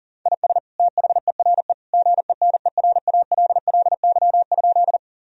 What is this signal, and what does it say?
Morse code, heard as a keyed tone, about 40 wpm, 700 Hz. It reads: IS THERE GENERALLY?
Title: Sentences from Top 1000 Words - Review Part 4 - 40wpm